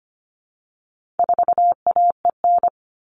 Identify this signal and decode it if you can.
Morse code, heard as a keyed tone, about 25 wpm, 700 Hz. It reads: 4AEN
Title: All Cut Numbers - 4-Digit 25wpm